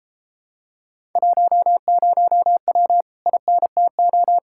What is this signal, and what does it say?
Morse code, heard as a keyed tone, about 33 wpm, 700 Hz. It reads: 10W INTO